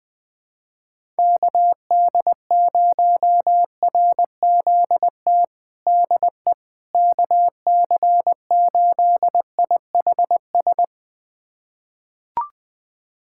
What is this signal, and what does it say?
Morse code, heard as a keyed tone, about 20 wpm, 690 Hz. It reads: KD0RZT DE KC8IHS
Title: USA Callsign DE Callsign 20wpm